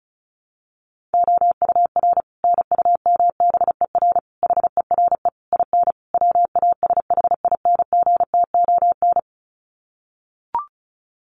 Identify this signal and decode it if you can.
Morse code, heard as a keyed tone, about 34 wpm, 700 Hz. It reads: OUR NUMBER HERE IN WASHINGTON
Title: Sets of 5 Words Encore2 35wpm